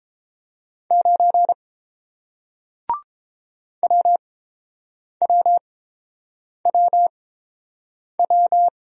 Morse code, heard as a keyed tone, about 29 wpm, spaced as for 7 wpm, 700 Hz. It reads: 9 WWWW